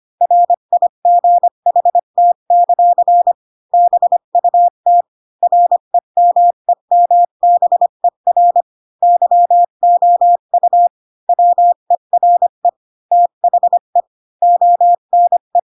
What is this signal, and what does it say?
Morse code, heard as a keyed tone, about 25 wpm, 700 Hz. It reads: RIGHT; BUT REMEMBER YOU WERE THE ONE